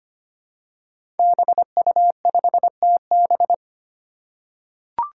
Morse code, heard as a keyed tone, about 25 wpm, 700 Hz. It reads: BU5TB
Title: Five Challenging Letter-Number Combinations 25wpm